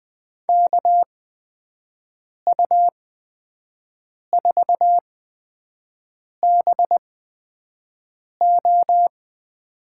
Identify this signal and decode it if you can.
Morse code, heard as a keyed tone, about 20 wpm, spaced as for 5 wpm, 700 Hz. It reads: KU4BO
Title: USA Call Signs - 5 Characters - 8x Character Spacing - 20wpm